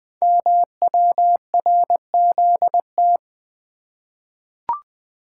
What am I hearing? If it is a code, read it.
Morse code, heard as a keyed tone, about 20 wpm, 700 Hz. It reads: MWRZT
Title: Code Groups - 5 Letters - 20wpm